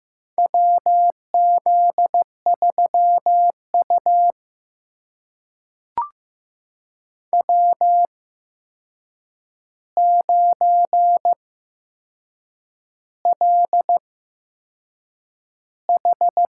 Morse code, heard as a keyed tone, about 15 wpm, spaced as for 6 wpm, 700 Hz. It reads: WZ3U W9LH
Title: USA Call Signs - 4 Characters - 8x Character Spacing - 15wpm